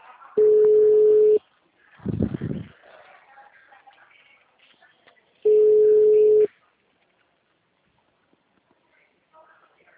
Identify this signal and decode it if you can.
Ringback tone (Europe, 425 Hz), with speech.